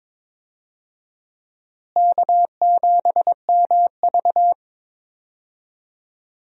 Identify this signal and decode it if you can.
Morse code, heard as a keyed tone, about 22 wpm, 700 Hz. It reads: K7MV
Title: USA Call Signs - 4 Characters - Repeated 2x - 22wpm